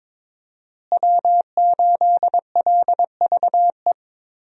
Morse code, heard as a keyed tone, about 22 wpm, 700 Hz. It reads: W8LVE
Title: USA Call Signs - 5 Characters - Repeated 3x - 22wpm